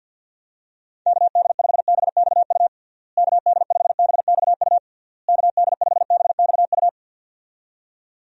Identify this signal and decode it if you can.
Morse code, heard as a keyed tone, about 53 wpm, 700 Hz. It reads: KD5BXU KD5BXU KD5BXU
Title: USA Call Signs - 6 Characters - Repeated 3x - 50wpm